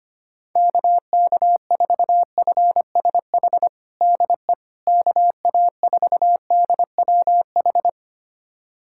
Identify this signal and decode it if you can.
Morse code, heard as a keyed tone, about 25 wpm, 700 Hz. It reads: KK4FSH DE KA4DWH
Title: USA Callsign DE Callsign - 6 Characters - 25wpm